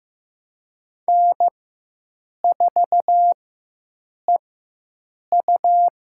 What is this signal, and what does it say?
Morse code, heard as a keyed tone, about 15 wpm, spaced as for 7 wpm, 700 Hz. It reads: N4EU
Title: USA Call Signs - 4x Character Spacing - 15wpm